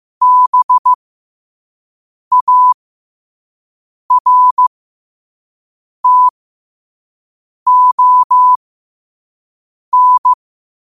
Morse code, heard as a keyed tone, about 15 wpm, spaced as for 5 wpm, 1000 Hz. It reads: BARTON
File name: qotdfiles / 5wpm